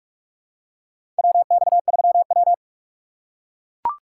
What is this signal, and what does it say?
Morse code, heard as a keyed tone, about 44 wpm, 700 Hz. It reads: WX3W
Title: USA Call Signs - 4 Characters - 6x Character Spacing - 45wpm